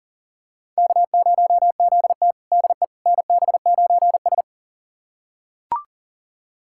Morse code, heard as a keyed tone, about 37 wpm, 700 Hz. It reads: K0ZT DE NB9S
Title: USA Callsign DE Callsign - 4 Characters - 40wpm